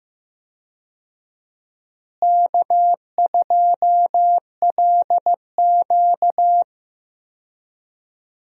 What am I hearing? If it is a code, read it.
Morse code, heard as a keyed tone, about 15 wpm, 700 Hz. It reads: K2LQ
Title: USA Call Signs - 4 Characters - Repeated 3x - 15wpm